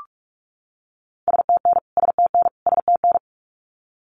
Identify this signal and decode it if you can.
Morse code, heard as a keyed tone, about 44 wpm, 700 Hz. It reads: STN STN STN
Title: Abbreviations Repeated 3x - 45wpm